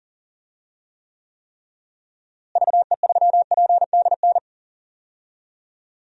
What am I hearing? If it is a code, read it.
Morse code, heard as a keyed tone, about 40 wpm, 700 Hz. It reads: VE3PDN
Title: DXCC-100 Call Signs - 6 Characters - Repeated 3x - 40wpm